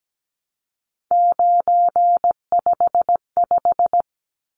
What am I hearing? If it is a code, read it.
Morse code, heard as a keyed tone, about 17 wpm, 700 Hz. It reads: 955
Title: Numbers - 3-Digit 17wpm